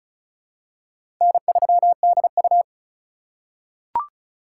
Morse code, heard as a keyed tone, about 36 wpm, 700 Hz. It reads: N3DU
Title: USA Call Signs - 4 Characters - 2x Character Spacing - 35wpm